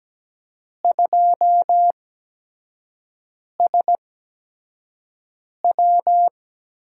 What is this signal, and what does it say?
Morse code, heard as a keyed tone, about 17 wpm, spaced as for 5 wpm, 700 Hz. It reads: 2SW
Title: USA Call Signs - 5 Characters - 8x Character Spacing - 17wpm